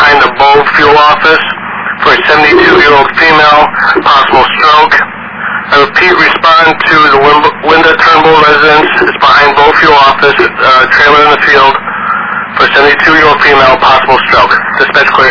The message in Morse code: none

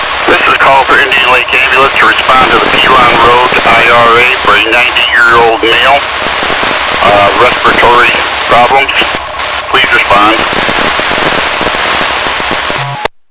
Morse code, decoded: none

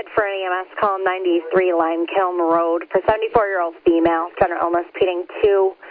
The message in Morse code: none

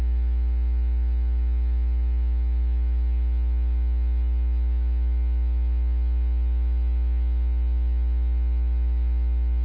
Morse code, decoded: none

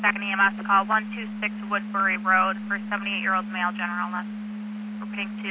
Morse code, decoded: none

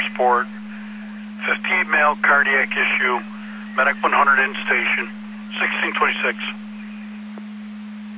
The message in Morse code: none

